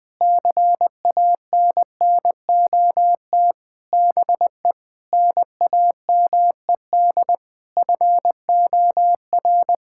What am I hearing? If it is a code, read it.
Morse code, heard as a keyed tone, about 20 wpm, 700 Hz. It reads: CANNOT BE NAMED FOR
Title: Sentences with 8-Word Phrases - 20wpm